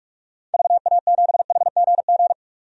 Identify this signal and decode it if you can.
Morse code, heard as a keyed tone, about 46 wpm, 700 Hz. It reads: VA7HGG
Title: DXCC-100 Call Signs - 6 Characters - 45wpm